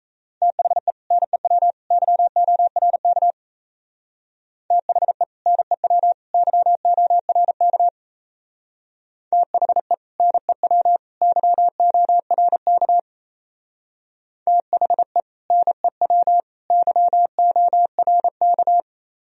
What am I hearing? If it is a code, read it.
Morse code, heard as a keyed tone, about 36 wpm, 700 Hz. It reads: THE NEW YORK THE NEW YORK THE NEW YORK THE NEW YORK